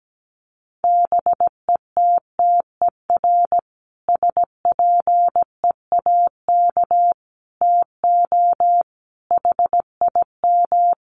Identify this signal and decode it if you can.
Morse code, heard as a keyed tone, about 17 wpm, 700 Hz. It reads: BETTER SPEAK TO HIM
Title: Sentences from Top 1100 Words 17wpm